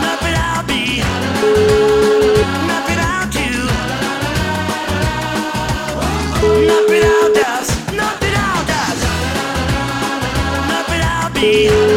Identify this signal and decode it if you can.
Ringback tone (Europe, 425 Hz), with noise.